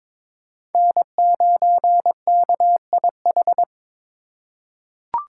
Morse code, heard as a keyed tone, about 22 wpm, 700 Hz. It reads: N9KIH
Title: USA Call Signs - 5 Characters - 2x Character Spacing - 22wpm